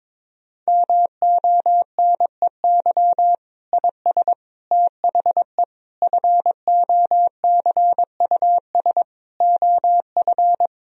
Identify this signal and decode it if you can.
Morse code, heard as a keyed tone, about 21 wpm, 700 Hz. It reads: MONEY IS THE FOCUS OF
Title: Sentences from Top 1000 Words - Review Part 4 - 22wpm